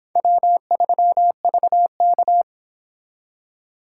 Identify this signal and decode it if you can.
Morse code, heard as a keyed tone, about 25 wpm, 700 Hz. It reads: W3VK